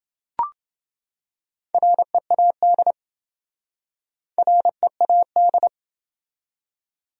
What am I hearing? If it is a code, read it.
Morse code, heard as a keyed tone, about 28 wpm, 700 Hz. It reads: READ READ